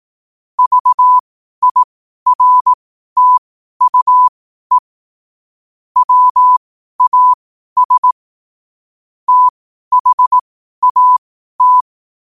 Morse code, heard as a keyed tone, about 18 wpm, spaced as for 12 wpm, 1000 Hz. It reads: VIRTUE WAS THAT
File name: qotdfiles / 12wpm